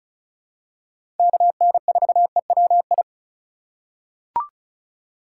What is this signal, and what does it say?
Morse code, heard as a keyed tone, about 36 wpm, 700 Hz. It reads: KN4EWI
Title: K1USN SST QSO Practice - 35wpm